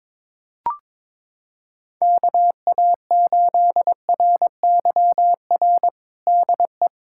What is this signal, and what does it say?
Morse code, heard as a keyed tone, about 21 wpm, 700 Hz. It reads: KA8RYR DE
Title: USA Callsign DE Callsign 22wpm